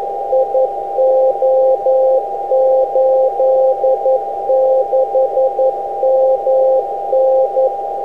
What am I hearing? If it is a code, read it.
Morse code, heard as a keyed tone, about 11 wpm, 520 Hz, with noise. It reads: IO86MN